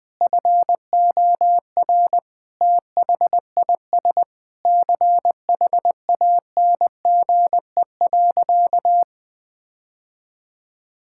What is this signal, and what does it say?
Morse code, heard as a keyed tone, about 21 wpm, 700 Hz. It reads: FOR THIS CHANGE.
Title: Sentences with 8-Word Phrases - 20wpm